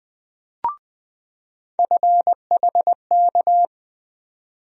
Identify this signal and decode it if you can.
Morse code, heard as a keyed tone, about 20 wpm, 700 Hz. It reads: FHK
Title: USA Call Sign Postfixes - 3 Letters 20wpm